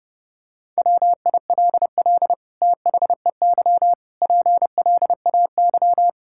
Morse code, heard as a keyed tone, about 30 wpm, 700 Hz. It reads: WILL THEY PLAY
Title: Sentences from Top 1000 Words - Review Part 6 - 30wpm